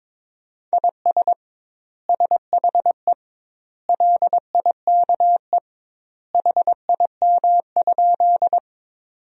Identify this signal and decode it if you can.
Morse code, heard as a keyed tone, about 22 wpm, 700 Hz. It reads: IS SHE LIKE HIM?
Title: 2x Word-Spacing - Sentences from Top 100 Words - 22wpm